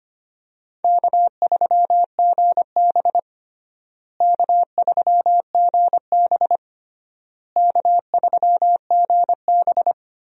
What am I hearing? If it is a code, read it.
Morse code, heard as a keyed tone, about 25 wpm, 700 Hz. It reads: K3GB K3GB K3GB
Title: USA Call Signs - 4 Characters - Repeated 3x - 25wpm